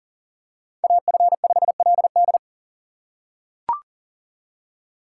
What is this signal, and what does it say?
Morse code, heard as a keyed tone, about 40 wpm, 700 Hz. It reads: AF5LD
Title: USA Call Signs - 3x Character Spacing - 40wpm